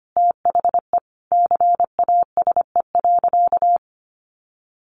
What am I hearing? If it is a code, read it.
Morse code, heard as a keyed tone, about 25 wpm, 700 Hz. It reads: THE CASE.
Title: Sentences from Top 1400 Words 25wpm